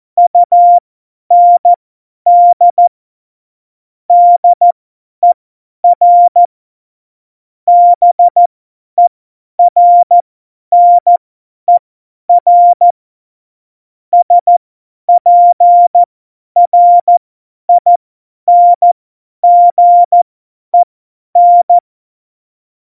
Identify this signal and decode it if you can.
Morse code, heard as a keyed tone, about 14 wpm, spaced as for 10 wpm, 700 Hz. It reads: UND DER BERNER SPRINGEN
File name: Morsekurs / Witze / Witz 24. November 2020, 50 F70